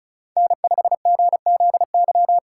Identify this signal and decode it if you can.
Morse code, heard as a keyed tone, about 42 wpm, 700 Hz. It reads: N5GZY
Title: Speed Racing - USA 1x3 Call Signs 35wpm